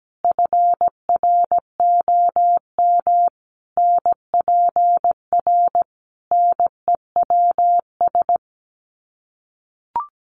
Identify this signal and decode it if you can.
Morse code, heard as a keyed tone, about 17 wpm, 700 Hz. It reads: FROM NPR NEWS
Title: Sets of 5 Words 17wpm